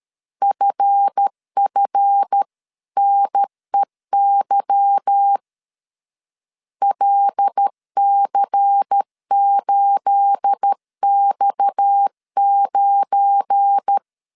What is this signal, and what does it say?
Morse code, heard as a keyed tone, about 13 wpm, 800 Hz. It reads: FFNEY LC8X9
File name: CW / DET / 1 9 1500C S C 12wpm 3811